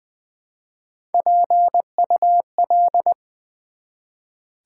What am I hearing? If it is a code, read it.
Morse code, heard as a keyed tone, about 20 wpm, 700 Hz. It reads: PUL
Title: Build A Word - 1 Letter At A Time - 4-Letter Words - 20wpm